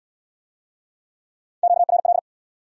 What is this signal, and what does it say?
Morse code, heard as a keyed tone, about 74 wpm, 700 Hz. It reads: QRL